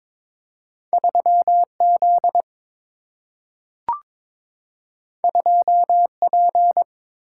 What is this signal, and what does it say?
Morse code, heard as a keyed tone, about 22 wpm, 700 Hz. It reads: 3 Z 2 P
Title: Single-Letter Single-Number 22wpm